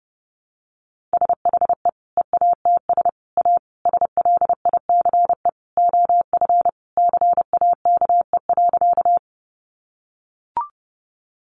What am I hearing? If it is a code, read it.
Morse code, heard as a keyed tone, about 30 wpm, 700 Hz. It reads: SHE EATS A SLICE OF CAKE.
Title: Additional Easy Phrases 30wpm